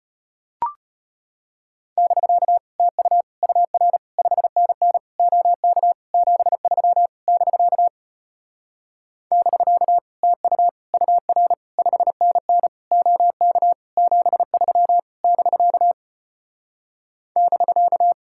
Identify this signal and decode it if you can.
Morse code, heard as a keyed tone, about 33 wpm, 700 Hz. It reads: <BK> TU UR 5NN OK 73 <BK> <BK> TU UR 5NN OK 73 <BK> <BK>